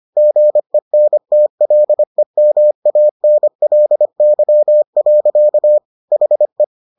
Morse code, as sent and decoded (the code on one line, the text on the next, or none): --. -. - .-.. -- .- -. .-.. -.-- .-.-.- / .... .
GENTLEMANLY. HE